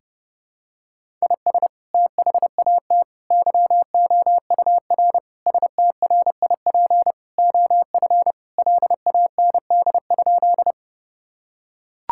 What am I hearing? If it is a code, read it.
Morse code, heard as a keyed tone, about 30 wpm, 700 Hz. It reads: IS THAT YOUR STRIP OF LAND?
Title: Sentences from Top 1600 Words 30wpm